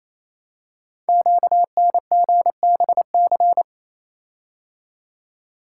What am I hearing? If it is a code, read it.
Morse code, heard as a keyed tone, about 27 wpm, 700 Hz. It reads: QNGBC
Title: Code Groups - 5 Letters - 28wpm